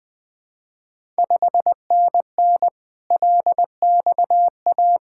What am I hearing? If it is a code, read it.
Morse code, heard as a keyed tone, about 20 wpm, 700 Hz. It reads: 5NN LXA
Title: Ontario QSO Party Exchange 20wpm